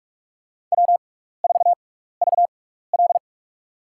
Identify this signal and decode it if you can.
Morse code, heard as a keyed tone, about 46 wpm, spaced as for 15 wpm, 700 Hz. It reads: W4VL
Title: USA Call Signs - 4 Characters - 6x Character Spacing - 45wpm